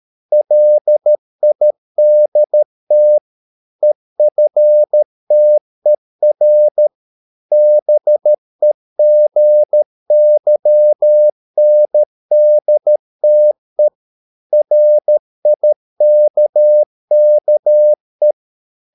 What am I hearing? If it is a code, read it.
Morse code, heard as a keyed tone, about 13 wpm, 600 Hz. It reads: LIDT EFTER BEGYNDTE RIKKE
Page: Ferie 13 wpm